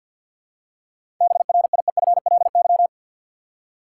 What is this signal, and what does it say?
Morse code, heard as a keyed tone, about 53 wpm, 700 Hz. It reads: BRIEFLY
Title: Seven Letter Words - 50wpm